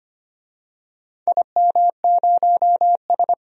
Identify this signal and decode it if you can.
Morse code, heard as a keyed tone, about 25 wpm, 700 Hz. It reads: IM0S
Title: DXCC-100 Call Signs - 4 Characters - 25wpm